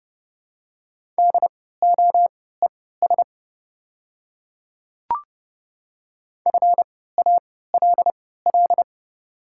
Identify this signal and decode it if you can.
Morse code, heard as a keyed tone, about 31 wpm, spaced as for 17 wpm, 700 Hz. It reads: DOES FALL
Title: 3x Character Spacing - Top 500 Words - 30wpm